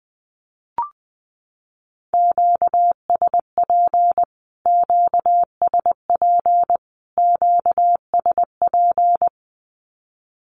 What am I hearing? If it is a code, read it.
Morse code, heard as a keyed tone, about 20 wpm, 700 Hz. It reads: QSP QSP QSP
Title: Q-Codes Repeated 3x 20wpm